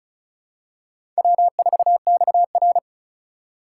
Morse code, heard as a keyed tone, about 34 wpm, 700 Hz. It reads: W4XR
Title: USA Call Signs - 4 Characters - 3x Character Spacing - 35wpm